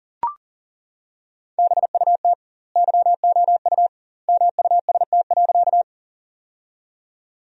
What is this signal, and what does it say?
Morse code, heard as a keyed tone, about 37 wpm, 700 Hz. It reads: BUT YOU MUST.
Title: 2x Word-Spacing - Sentences from Top 200 Words - 40wpm